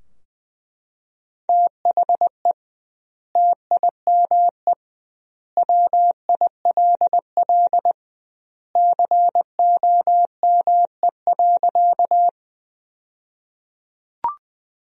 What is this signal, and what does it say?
Morse code, heard as a keyed tone, about 20 wpm, 700 Hz. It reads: THE TIME WILL COME.
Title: 2x Word-Spacing - Sentences from Top 100 Words - 20wpm